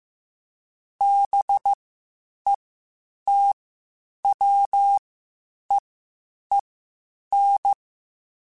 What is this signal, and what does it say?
Morse code, heard as a keyed tone, about 15 wpm, spaced as for 8 wpm, 800 Hz. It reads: BETWEEN
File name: Spaced Characters / Spaced Characters-mp3s / Weather 5 12 sp @13